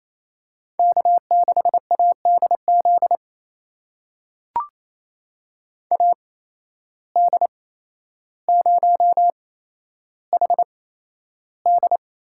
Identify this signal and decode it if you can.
Morse code, heard as a keyed tone, about 28 wpm, spaced as for 12 wpm, 700 Hz. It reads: K6ADZ AD0HD
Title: USA Call Signs - 5 Characters - 8x Character Spacing - 28wpm